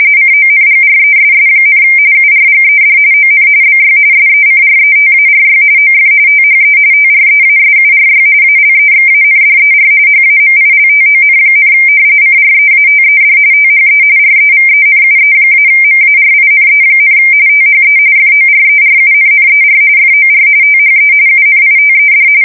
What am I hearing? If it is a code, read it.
RTTY, about 45.45 baud. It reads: CONGRESS, URGED ON BY SECRETARY OF THE TREASURY ALEXANDER HAMILTON, CREATED THE REVENUE-MARINE, THE FORERUNNER FOR THE UNITED
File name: fsk170 / usnavy1